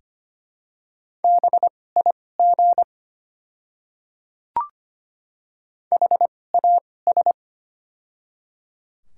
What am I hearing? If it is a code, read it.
Morse code, heard as a keyed tone, about 25 wpm, spaced as for 18 wpm, 700 Hz. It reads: BIG HAS
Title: 2x Character Spacing - Top 300 Words - 25wpm